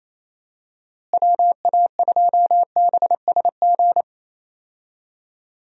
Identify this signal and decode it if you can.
Morse code, heard as a keyed tone, about 27 wpm, 700 Hz. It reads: WA2BSG
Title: USA Call Signs - 6 Characters - Repeated 2x - 28wpm